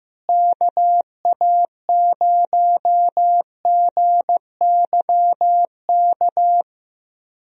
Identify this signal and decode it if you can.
Morse code, heard as a keyed tone, about 15 wpm, 700 Hz. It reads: KA0GYK
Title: USA Call Signs - 6 Characters - Repeated 2x - 15wpm